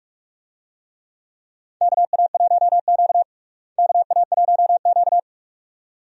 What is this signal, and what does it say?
Morse code, heard as a keyed tone, about 42 wpm, 700 Hz. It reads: KA1Q KA1Q
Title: USA Call Signs - 4 Characters - Repeated 2x - 45wpm